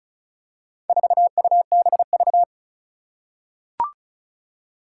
Morse code, heard as a keyed tone, about 36 wpm, 700 Hz. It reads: 4UBV
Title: Four Challenging Letter-Number Combinations 35wpm